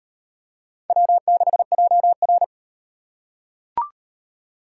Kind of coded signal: Morse code

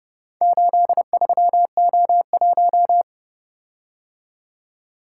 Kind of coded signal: Morse code

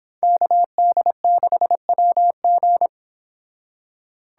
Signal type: Morse code

